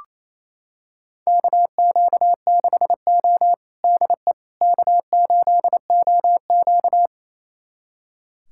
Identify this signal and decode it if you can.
Morse code, heard as a keyed tone, about 28 wpm, 700 Hz. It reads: KQ6O DE K8OQ
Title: USA Callsign DE Callsign - 4 Characters - 28wpm